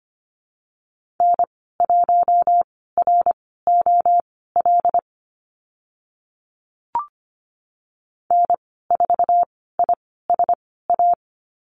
Morse code, heard as a keyed tone, about 25 wpm, spaced as for 16 wpm, 690 Hz. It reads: N1ROL N4ISA
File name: Lesson 15g - Number 5 - Call Signs 25wpm